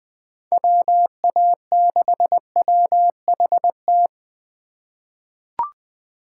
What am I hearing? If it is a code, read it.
Morse code, heard as a keyed tone, about 20 wpm, 700 Hz. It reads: WA6WHT
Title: USA Call Signs - 6 Characters - 3x Character Spacing - 20wpm